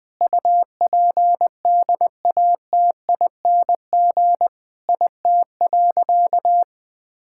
Morse code, heard as a keyed tone, about 21 wpm, 700 Hz. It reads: UPDATING IT.
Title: Sentences with 6-Word Phrases - 20wpm